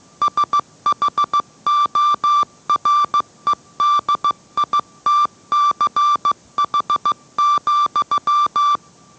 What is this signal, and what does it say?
Morse code, heard as a keyed tone, about 15 wpm, 1170 Hz. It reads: SHOREDITCH,